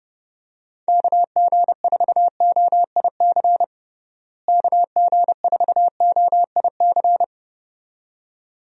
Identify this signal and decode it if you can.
Morse code, heard as a keyed tone, about 30 wpm, 700 Hz. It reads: KG4OIC KG4OIC
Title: USA Call Signs - 6 Characters - Repeated 2x - 30wpm